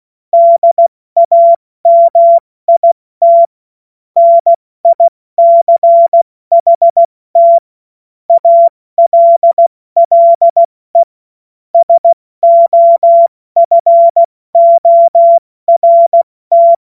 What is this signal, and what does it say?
Morse code, heard as a keyed tone, about 16 wpm, spaced as for 14 wpm, 680 Hz. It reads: DAMIT NICHT ALLE SOFORT
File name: Morsekurs / Witze / Witz 27 Mai 2025 70 F80